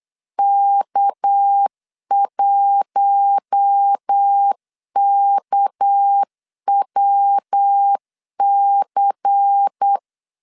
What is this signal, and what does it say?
Morse code, heard as a keyed tone, about 8 wpm, 800 Hz. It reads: K1KWC
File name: LEZIONI / 6 PKX0 / 6 PK 8wpm 3003